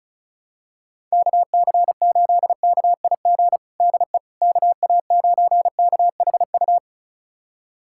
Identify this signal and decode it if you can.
Morse code, heard as a keyed tone, about 36 wpm, 700 Hz. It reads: KC8KIG DE KA9KHU
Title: USA Callsign DE Callsign - 6 Characters - 35wpm